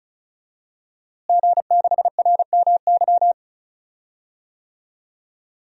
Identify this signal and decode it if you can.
Morse code, heard as a keyed tone, about 36 wpm, 700 Hz. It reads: G6RMY
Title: DXCC-100 Call Signs - 5 Characters - Repeated 2x - 35wpm